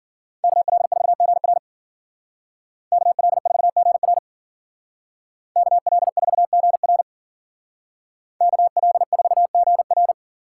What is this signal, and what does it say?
Morse code, heard as a keyed tone, about 52 wpm, 700 Hz. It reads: KL4GR KL4GR KL4GR KL4GR